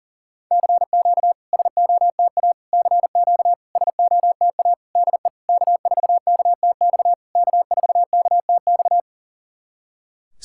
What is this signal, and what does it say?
Morse code, heard as a keyed tone, about 40 wpm, 690 Hz. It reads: CQ SOTA CQ SOTA DE K4KTX K4KTX
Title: SOTA QSO Practice 40wpm